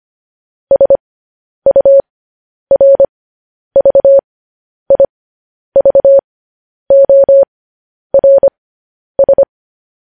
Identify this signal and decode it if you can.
Morse code, heard as a keyed tone, about 25 wpm, spaced as for 10 wpm, 550 Hz. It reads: SURVIVORS
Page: news-headlines-in-morse-code-at-10-wpm